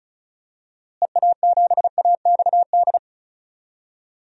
Morse code, heard as a keyed tone, about 36 wpm, 700 Hz. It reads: EA7AXD
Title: DXCC-100 Call Signs - 6 Characters - Repeated 3x - 35wpm